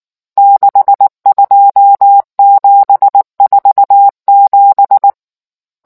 Morse code, heard as a keyed tone, about 19 wpm, 800 Hz. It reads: 62747